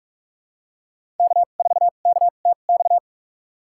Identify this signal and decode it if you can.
Morse code, heard as a keyed tone, about 46 wpm, spaced as for 32 wpm, 700 Hz. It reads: K4KTX
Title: USA Call Signs - 2x Character Spacing - 45wpm